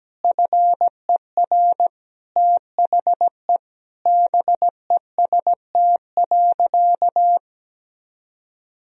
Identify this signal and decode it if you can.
Morse code, heard as a keyed tone, about 17 wpm, 700 Hz. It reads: FER THE BEST.
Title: Sentences from Top 500 Words and Abbreviations 17wpm